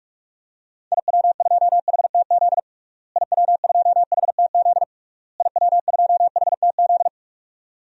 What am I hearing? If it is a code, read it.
Morse code, heard as a keyed tone, about 46 wpm, 700 Hz. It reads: IW2HTZ IW2HTZ IW2HTZ
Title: DXCC-100 Call Signs - 6 Characters - Repeated 3x - 45wpm